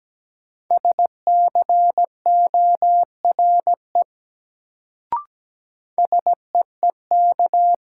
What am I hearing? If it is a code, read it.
Morse code, heard as a keyed tone, about 17 wpm, 700 Hz. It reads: SCORE SEEK
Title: Sentences from Top 1600 Words - New Word Drill Alt - 17wpm